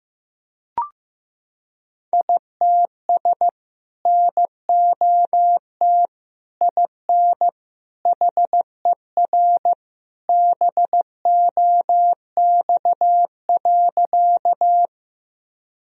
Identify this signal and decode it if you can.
Morse code, heard as a keyed tone, about 15 wpm, 700 Hz. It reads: ITS NOT IN HER BOX.
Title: Sentences from Top 1000 Words - Review Part 2 - 15wpm